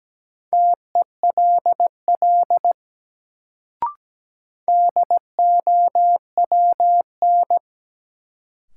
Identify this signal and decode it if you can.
Morse code, heard as a keyed tone, about 17 wpm, 700 Hz. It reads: TELL DOWN
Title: Sets of 2 Words - New Word Drill - 17wpm